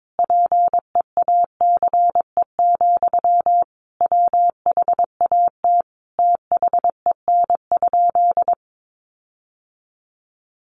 Morse code, heard as a keyed tone, about 22 wpm, 700 Hz. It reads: PEACE, WHAT THEN?
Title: Sentences from Top 1500 Words 22wpm